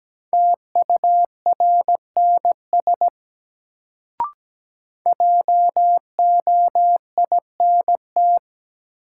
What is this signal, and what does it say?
Morse code, heard as a keyed tone, about 17 wpm, 700 Hz. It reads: TURNS JOINT
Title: Sets of 5 Words Encore - New Word Drill - 17wpm